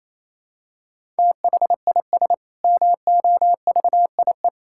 Morse code, heard as a keyed tone, about 28 wpm, 700 Hz. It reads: THIS MOVIE
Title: Sentences from Top 1000 Words - Review Part 5 - 28wpm